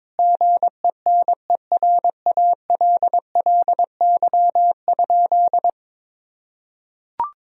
Morse code, heard as a keyed tone, about 22 wpm, 700 Hz. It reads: GENERALLY?
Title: Sentences from Top 1000 Words - Review Part 4 - 22wpm